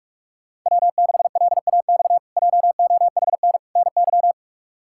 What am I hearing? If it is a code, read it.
Morse code, heard as a keyed tone, about 43 wpm, 700 Hz. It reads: W6LAX JOHN NY
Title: K1USN SST QSO Practice - 45wpm